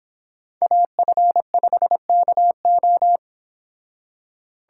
Morse code, heard as a keyed tone, about 28 wpm, 700 Hz. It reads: AF5KO